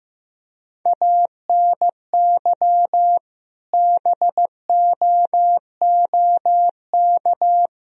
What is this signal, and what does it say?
Morse code, heard as a keyed tone, about 15 wpm, 700 Hz. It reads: ANY BOOK
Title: Sentences from Top 300 Words and Abbreviations 15wpm